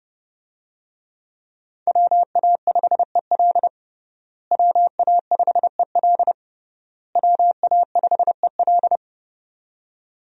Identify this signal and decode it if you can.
Morse code, heard as a keyed tone, about 30 wpm, 700 Hz. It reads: WA5EL WA5EL WA5EL
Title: USA Call Signs - 5 Characters - Repeated 3x - 30wpm